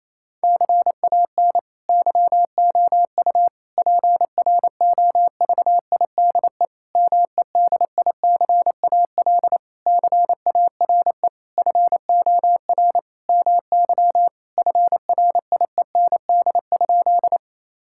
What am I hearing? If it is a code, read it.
Morse code, heard as a keyed tone, about 28 wpm, 700 Hz. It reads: CAN YOU PROVIDE MEDICAL CARE FOR MY FRIEND?
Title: Sentences from Top 1000 Words - Review Part 5 - 28wpm